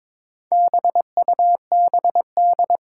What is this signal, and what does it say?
Morse code, heard as a keyed tone, about 22 wpm, 700 Hz. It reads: BUBD
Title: Four Challenging Letter-Number Combinations 22wpm